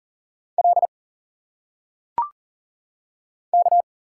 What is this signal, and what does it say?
Morse code, heard as a keyed tone, about 39 wpm, 700 Hz. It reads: L K
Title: Single Letters - Phonetic Pronunciation 40wpm